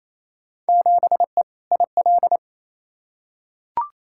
Morse code, heard as a keyed tone, about 28 wpm, 700 Hz. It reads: 7E IL
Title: Field Day Exchanges Encore - 28wpm